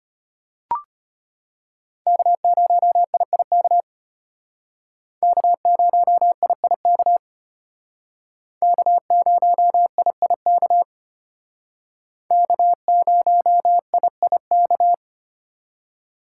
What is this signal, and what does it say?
Morse code, heard as a keyed tone, about 32 wpm, 700 Hz. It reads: K0IIK K0IIK K0IIK K0IIK